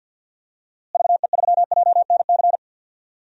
Morse code, heard as a keyed tone, about 47 wpm, 700 Hz. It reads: VE3JNC
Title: DXCC-100 Call Signs - 6 Characters - Repeated 2x - 50wpm